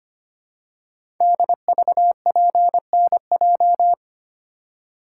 Morse code, heard as a keyed tone, about 25 wpm, 700 Hz. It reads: DVPNJ
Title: Code Groups - 5 Letters - 25wpm